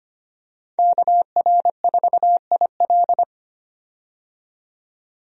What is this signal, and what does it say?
Morse code, heard as a keyed tone, about 25 wpm, 700 Hz. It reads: KR4IL
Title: USA Call Signs - 5 Characters - 25wpm